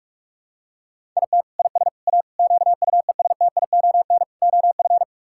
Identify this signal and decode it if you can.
Morse code, heard as a keyed tone, about 46 wpm, 700 Hz. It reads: IT IS A QUESTION OF
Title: Sentences from Top 1000 Words - Review Part 6 - 45wpm